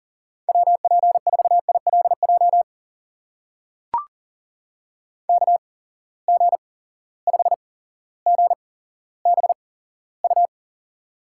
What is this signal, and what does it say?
Morse code, heard as a keyed tone, about 39 wpm, spaced as for 17 wpm, 700 Hz. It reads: WP4ILJ KG5GBU
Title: USA Call Signs - 6 Characters - 8x Character Spacing - 40wpm